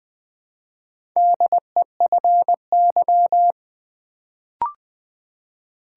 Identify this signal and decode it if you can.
Morse code, heard as a keyed tone, about 20 wpm, 700 Hz. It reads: DEFY
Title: Build A Word - 1 Letter At A Time - 4-Letter Words - 20wpm